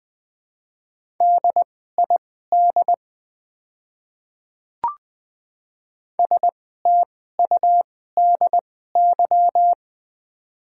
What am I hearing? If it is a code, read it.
Morse code, heard as a keyed tone, about 20 wpm, spaced as for 15 wpm, 690 Hz. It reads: DID STUDY
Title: 2x Character Spacing - Top 300 Words - 20wpm